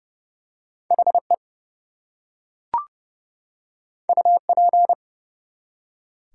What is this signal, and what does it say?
Morse code, heard as a keyed tone, about 30 wpm, 700 Hz. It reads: HE UP
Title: Two Letter Words - 30wpm